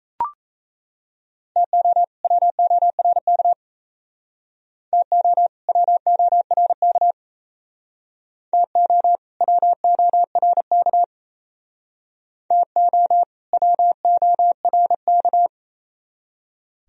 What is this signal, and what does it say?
Morse code, heard as a keyed tone, about 35 wpm, 690 Hz. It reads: TO WORK TO WORK TO WORK TO WORK